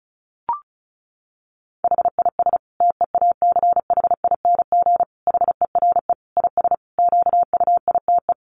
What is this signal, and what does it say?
Morse code, heard as a keyed tone, about 34 wpm, 700 Hz. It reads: HIS TEACHING HERE IS QUITE
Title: Sentences from Top 1100 Words 35wpm